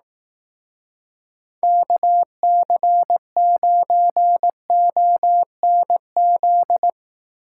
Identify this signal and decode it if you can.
Morse code, heard as a keyed tone, about 18 wpm, 700 Hz. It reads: KC9ONZ
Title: Speed Racing - USA 2x3 Call Signs 15wpm